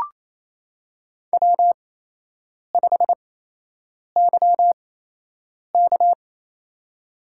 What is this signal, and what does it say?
Morse code, heard as a keyed tone, about 28 wpm, spaced as for 8 wpm, 700 Hz. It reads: W5YK
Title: USA Call Signs - 8x Character Spacing - 28wpm